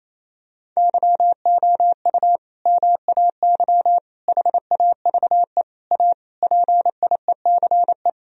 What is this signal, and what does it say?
Morse code, heard as a keyed tone, about 27 wpm, 700 Hz. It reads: YOU MAY HAVE A PIECE
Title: Sentences from Top 1000 Words - Review Part 2 - 28wpm